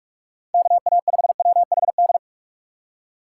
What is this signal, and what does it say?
Morse code, heard as a keyed tone, about 45 wpm, 700 Hz. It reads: KA5WHD
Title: USA Call Signs - 6 Characters - 4x Character Spacing - 45wpm